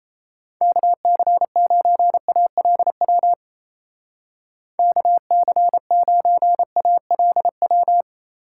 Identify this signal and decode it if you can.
Morse code, heard as a keyed tone, about 37 wpm, 700 Hz. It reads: KC9ALW KC9ALW